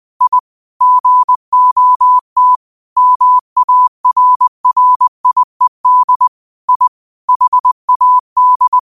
Morse code, heard as a keyed tone, about 20 wpm, 1000 Hz. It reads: I GOT MARRIED I HAD